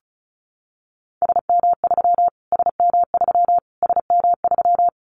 Morse code, heard as a keyed tone, about 34 wpm, 700 Hz. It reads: SM3 SM3 SM3
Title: Callsign Prefixes for Top 25 DXCC - Repeated 3x - 35wpm